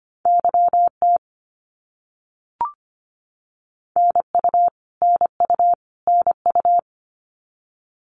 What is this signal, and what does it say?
Morse code, heard as a keyed tone, about 25 wpm, 700 Hz. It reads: YT NU NU NU
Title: Canadian Provinces and Territories - Repeated 3x - 25wpm